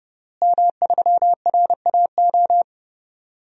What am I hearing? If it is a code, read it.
Morse code, heard as a keyed tone, about 30 wpm, 700 Hz. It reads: M3RAO
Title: DXCC-100 Call Signs - 5 Characters - Repeated 3x - 30wpm